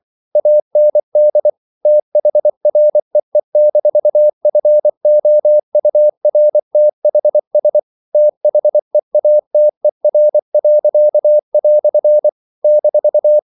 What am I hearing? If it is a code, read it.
Morse code, heard as a keyed tone, about 24 wpm, 600 Hz. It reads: AND THREE-FOURTHS THEATER." -
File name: CW / Practice / EnglishQuotations0324